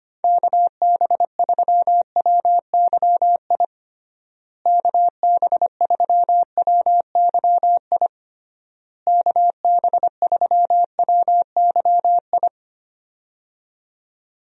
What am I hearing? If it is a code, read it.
Morse code, heard as a keyed tone, about 25 wpm, 700 Hz. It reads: KB3WYI KB3WYI KB3WYI
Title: USA Call Signs - 6 Characters - Repeated 3x - 25wpm